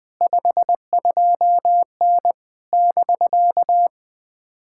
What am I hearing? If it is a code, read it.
Morse code, heard as a keyed tone, about 20 wpm, 700 Hz. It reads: 52N <BK>
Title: POTA QSO Practice - 20wpm